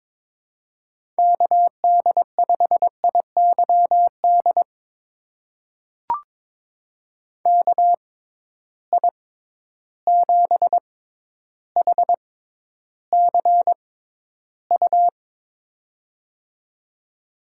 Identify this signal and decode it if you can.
Morse code, heard as a keyed tone, about 22 wpm, spaced as for 11 wpm, 700 Hz. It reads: KD5IYD KI7HCU
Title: USA Call Signs - 6 Characters - 6x Character Spacing - 22wpm